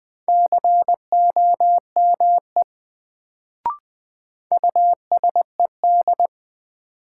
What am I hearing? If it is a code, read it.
Morse code, heard as a keyed tone, about 20 wpm, 700 Hz. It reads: COME USED
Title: Sets of 2 Words - New Word Drill - 20wpm